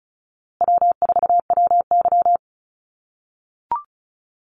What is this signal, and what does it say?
Morse code, heard as a keyed tone, about 34 wpm, 700 Hz. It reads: W4WY
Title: USA Call Signs Repeated 3x 35wpm